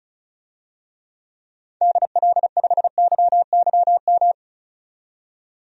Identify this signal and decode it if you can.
Morse code, heard as a keyed tone, about 36 wpm, 690 Hz. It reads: DL5YYM
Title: DXCC-100 Call Signs - 6 Characters - 35wpm